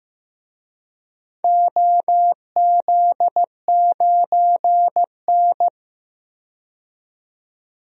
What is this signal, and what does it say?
Morse code, heard as a keyed tone, about 15 wpm, 700 Hz. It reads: OZ9N
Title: DXCC-100 Call Signs - 4 Characters - 15wpm